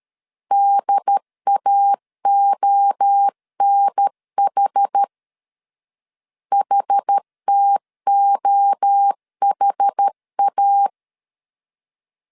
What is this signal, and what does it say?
Morse code, heard as a keyed tone, about 13 wpm, 800 Hz. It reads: DAONH HTOHA
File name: CW / DET / 1 3 1500C S A 12wpm 2905